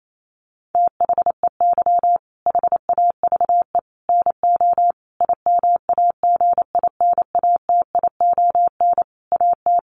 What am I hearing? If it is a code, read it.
Morse code, heard as a keyed tone, about 28 wpm, 700 Hz. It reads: THEY HAVE NO IMAGINATION AT
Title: Sentences from Top 1300 Words 28wpm